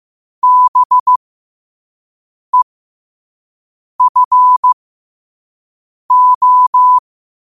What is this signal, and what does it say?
Morse code, heard as a keyed tone, about 15 wpm, spaced as for 5 wpm, 1000 Hz. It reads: BEFO
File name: qotdfiles / 5wpm